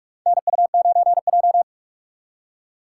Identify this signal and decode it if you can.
Morse code, heard as a keyed tone, about 42 wpm, 700 Hz. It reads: NU9J
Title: USA Call Signs - 2x Character Spacing - 45wpm